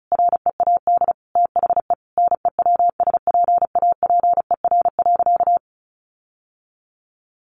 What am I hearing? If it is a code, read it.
Morse code, heard as a keyed tone, about 34 wpm, 700 Hz. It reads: READ THE NEWSPAPER.
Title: Sentences from Top 1400 Words 35wpm